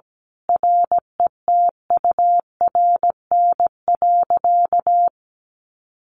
Morse code, heard as a keyed tone, about 17 wpm, 700 Hz. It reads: RETURN.
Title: Sentences from Top 1100 Words 17wpm